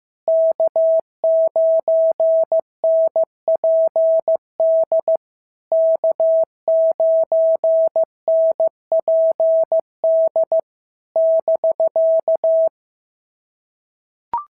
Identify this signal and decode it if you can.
Morse code, heard as a keyed tone, about 14 wpm, 650 Hz. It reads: K9NPD K9NPD <BK>
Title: SOTA QSO Practice 15wpm